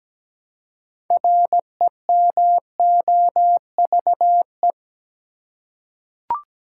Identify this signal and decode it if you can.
Morse code, heard as a keyed tone, about 17 wpm, 700 Hz. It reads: REMOVE
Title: Six Letter Words - 17wpm